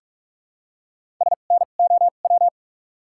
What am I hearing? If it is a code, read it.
Morse code, heard as a keyed tone, about 43 wpm, spaced as for 31 wpm, 700 Hz. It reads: SNOW
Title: 2x Character Spacing - Top 500 Words - 45wpm